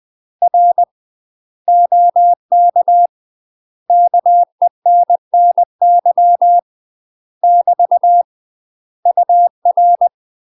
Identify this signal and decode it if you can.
Morse code, heard as a keyed tone, about 20 wpm, 700 Hz. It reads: R OK KENNY <BT> UR